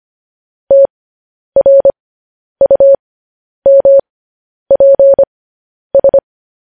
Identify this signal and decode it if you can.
Morse code, heard as a keyed tone, about 25 wpm, spaced as for 10 wpm, 550 Hz. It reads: TRUMPS
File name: podcast / News headlines-1x-25x10